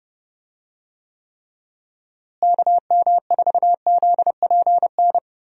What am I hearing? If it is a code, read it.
Morse code, heard as a keyed tone, about 31 wpm, 700 Hz. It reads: KM4ZPN
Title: USA Call Signs - 6 Characters - Repeated 2x - 30wpm